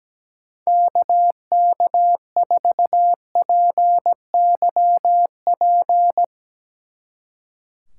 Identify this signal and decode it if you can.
Morse code, heard as a keyed tone, about 18 wpm, 700 Hz. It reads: KK4PYP
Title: Speed Racing - USA Call Signs 17wpm